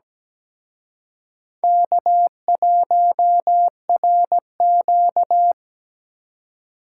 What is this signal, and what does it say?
Morse code, heard as a keyed tone, about 21 wpm, 700 Hz. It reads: K1RQ
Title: Speed Racing - USA Call Signs 17wpm